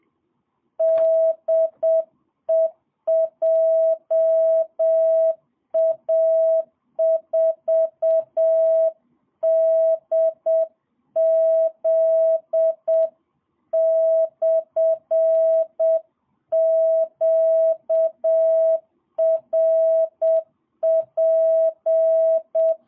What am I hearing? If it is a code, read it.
Morse code, heard as a keyed tone, about 7 wpm, 650 Hz. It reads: DE JA4DZ/QRP
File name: jh8cbh / 2022cw